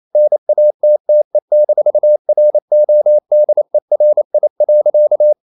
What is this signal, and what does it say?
Morse code, heard as a keyed tone, about 28 wpm, 600 Hz. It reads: NATTE-RODERI.
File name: Ferie QRQ 28wpm